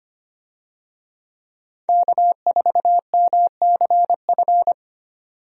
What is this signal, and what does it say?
Morse code, heard as a keyed tone, about 25 wpm, 700 Hz. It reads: K4MCF
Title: USA Call Signs - 5 Characters - 25wpm